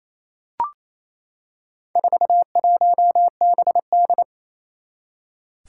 Morse code, heard as a keyed tone, about 27 wpm, 700 Hz. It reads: 41BD
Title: Four Challenging Letter-Number Combinations 28wpm